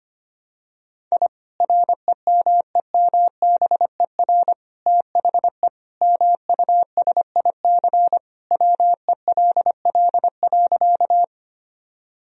Morse code, heard as a keyed tone, about 25 wpm, 700 Hz. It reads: I REMEMBER THE MUSIC WELL.
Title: Sentences from Top 1000 Words - Review Part 1 - 25wpm